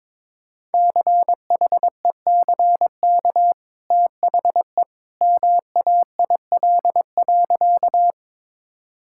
Morse code, heard as a keyed tone, about 22 wpm, 700 Hz. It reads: CHECK THE MAIL.
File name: practice / 22wpm / Easy Sentences with Hints - 22wpm